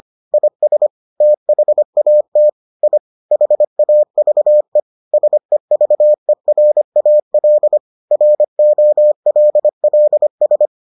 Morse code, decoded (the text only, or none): IS THAT I HAVE SEVERAL ROLLS